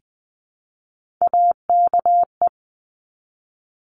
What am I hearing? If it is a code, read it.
Morse code, heard as a keyed tone, about 20 wpm, 700 Hz. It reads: AKE
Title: Top Three Letter Combinations in Words 20wpm